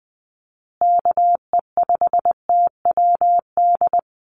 Morse code, heard as a keyed tone, about 20 wpm, 700 Hz. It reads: KE5TWD
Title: USA Call Signs Repeated 3x 20wpm